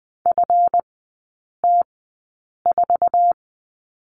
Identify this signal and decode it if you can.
Morse code, heard as a keyed tone, about 20 wpm, spaced as for 8 wpm, 700 Hz. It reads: FT4
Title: 166 Lesson 27e - Number 7 - Three Characters 20wpm